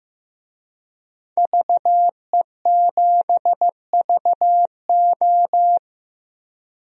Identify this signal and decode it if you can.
Morse code, heard as a keyed tone, about 15 wpm, 700 Hz. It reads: VE7VO
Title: DXCC-100 Call Signs - 5 Characters - 15wpm